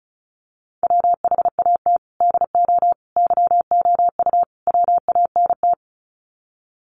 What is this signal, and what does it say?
Morse code, heard as a keyed tone, about 34 wpm, 700 Hz. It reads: WHAT DO YOU WANT
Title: Sets of 4 Words 35wpm